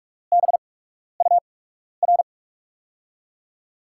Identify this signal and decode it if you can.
Morse code, heard as a keyed tone, about 44 wpm, spaced as for 12 wpm, 700 Hz. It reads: BUR
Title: USA Call Signs - 5 Characters - 8x Character Spacing - 45wpm